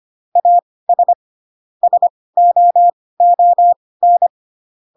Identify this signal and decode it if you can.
Morse code, heard as a keyed tone, about 25 wpm, spaced as for 18 wpm, 700 Hz. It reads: AS SOON